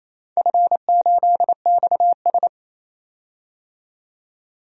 Morse code, heard as a keyed tone, about 28 wpm, 700 Hz. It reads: F8XS
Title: DXCC-100 Call Signs - 4 Characters - 28wpm